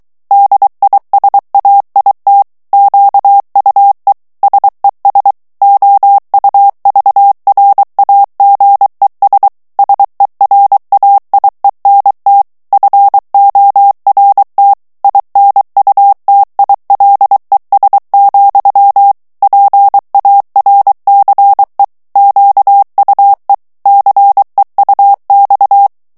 Morse code, heard as a keyed tone, about 23 wpm, 800 Hz. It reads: DISAIT QUE SES OUVRAGES SERAIENT FORT INUTILES, PARCE QUE CEUX
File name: morse mp3 / FRANCAIS TEXTE CLAIR 2 VITESSE 22 MOTS MINUTES